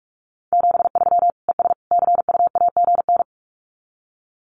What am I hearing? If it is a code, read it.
Morse code, heard as a keyed tone, about 44 wpm, 700 Hz. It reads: 73 ES CUAGN
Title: Common QSO Phrases 45wpm